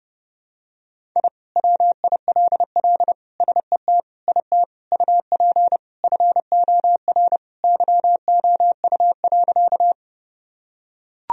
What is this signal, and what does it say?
Morse code, heard as a keyed tone, about 32 wpm, 700 Hz. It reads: I WILL SET IT UP FOR YOU.
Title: Sentences from Top 1000 Words - Review Part 1 - 30wpm